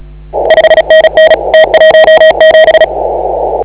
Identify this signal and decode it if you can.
Morse code, heard as a keyed tone, about 33 wpm, 1900 Hz, with noise. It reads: 5NN T17
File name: AUDIO2019 / HA8KAZ CW